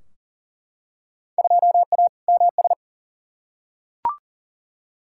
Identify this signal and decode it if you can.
Morse code, heard as a keyed tone, about 40 wpm, 700 Hz. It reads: 2A MS
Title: Field Day Exchanges - 40wpm